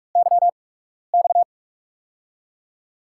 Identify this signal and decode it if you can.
Morse code, heard as a keyed tone, about 46 wpm, spaced as for 12 wpm, 700 Hz. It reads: YX
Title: USA Call Signs - 8x Character Spacing - 45wpm